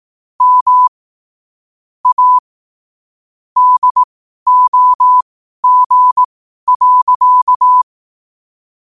Morse code, heard as a keyed tone, about 18 wpm, spaced as for 12 wpm, 1000 Hz. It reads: M A DOG.
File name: qotdfiles / 12wpm